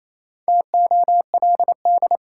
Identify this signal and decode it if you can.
Morse code, heard as a keyed tone, about 28 wpm, 700 Hz. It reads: TOLD
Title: Sentences from Top 400 Words - New Word Drill - 28wpm